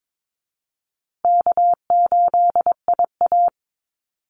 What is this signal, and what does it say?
Morse code, heard as a keyed tone, about 22 wpm, 700 Hz. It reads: K8IA
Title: Easy Call Signs Repeated 3x - 22wpm